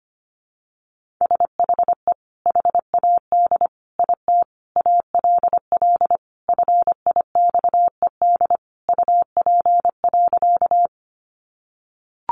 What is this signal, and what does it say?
Morse code, heard as a keyed tone, about 25 wpm, 700 Hz. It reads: SHE HAD IT ALL FIXED UP.
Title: Sentences from Top 1100 Words 25wpm